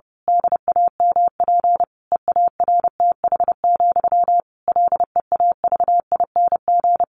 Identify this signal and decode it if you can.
Morse code, heard as a keyed tone, about 30 wpm, 700 Hz. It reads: DAMP EARTH, LEAVING
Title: A Sound Of Thunder 30wpm